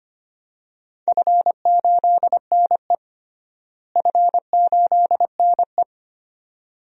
Morse code, heard as a keyed tone, about 25 wpm, 700 Hz. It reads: F8NE F8NE
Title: DXCC-100 Call Signs - 4 Characters - Repeated 2x - 25wpm